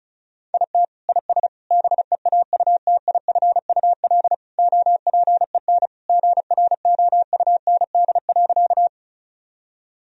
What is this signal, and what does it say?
Morse code, heard as a keyed tone, about 36 wpm, 700 Hz. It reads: IT IS BEAUTIFUL OPEN GROUND.
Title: Sentences from Top 1000 Words - Review Part 2 - 35wpm